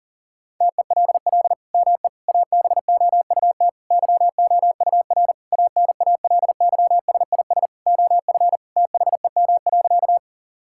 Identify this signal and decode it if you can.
Morse code, heard as a keyed tone, about 40 wpm, 700 Hz. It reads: TELL ME ABOUT YOUR ANALYSIS OF THEM.
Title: Sentences from Top 1000 Words - Review Part 5 - 40wpm